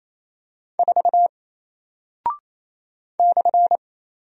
Morse code, heard as a keyed tone, about 27 wpm, 700 Hz. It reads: 4 /
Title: Single Character (Letter:Number:Punctuation:Prosign) Alt 28wpm